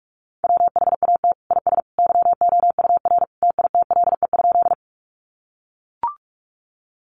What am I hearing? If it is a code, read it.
Morse code, heard as a keyed tone, about 44 wpm, 700 Hz. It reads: WHAT IS YOUR TITLE?
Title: Sentences from Top 1200 Words 45wpm